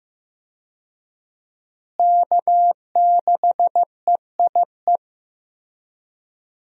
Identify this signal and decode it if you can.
Morse code, heard as a keyed tone, about 15 wpm, 700 Hz. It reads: K6EIE
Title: USA Call Signs - 5 Characters - Repeated 3x - 15wpm